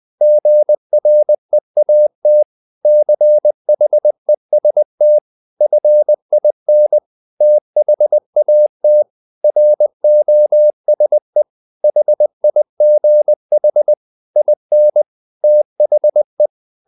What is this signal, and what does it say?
Morse code, heard as a keyed tone, about 20 wpm, 600 Hz. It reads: GREAT CHEST FIN THAT ROSE HIGH IN THE
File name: the old man and the sea part 2 20wpm